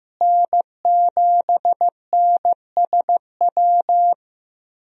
Morse code, heard as a keyed tone, about 15 wpm, 700 Hz. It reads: N7NSW
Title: USA Call Signs - 5 Characters - Repeated 2x - 15wpm